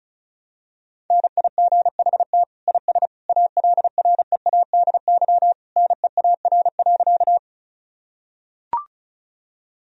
Morse code, heard as a keyed tone, about 34 wpm, 700 Hz. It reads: NIGHT IS ALREADY NEAR.
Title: Sentences from Top 1000 Words - Review Part 1 - 35wpm